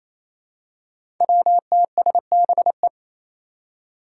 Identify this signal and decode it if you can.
Morse code, heard as a keyed tone, about 28 wpm, 700 Hz. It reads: WTSBE
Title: Five Challenging Letter-Number Combinations 28wpm